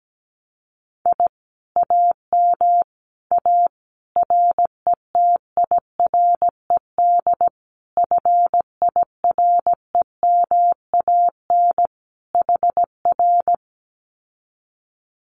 Morse code, heard as a keyed tone, about 17 wpm, 700 Hz. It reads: I AM A RETIRED FIREMAN HR
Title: Common QSO Phrases Encore 17wpm